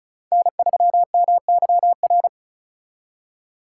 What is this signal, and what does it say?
Morse code, heard as a keyed tone, about 45 wpm, 700 Hz. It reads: N3MYR